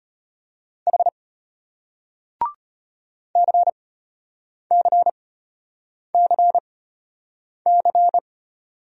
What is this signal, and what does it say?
Morse code, heard as a keyed tone, about 34 wpm, spaced as for 8 wpm, 700 Hz. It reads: H CCCC